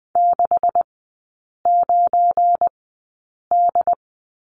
Morse code, heard as a keyed tone, about 20 wpm, spaced as for 8 wpm, 700 Hz. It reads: 69D
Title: 172 Lesson 28e - Number 9 - Three Characters 20wpm